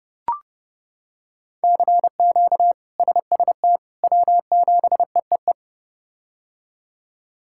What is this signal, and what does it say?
Morse code, heard as a keyed tone, about 30 wpm, 700 Hz. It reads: CQ SST W7EEE
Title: K1USN SST QSO Practice - 30wpm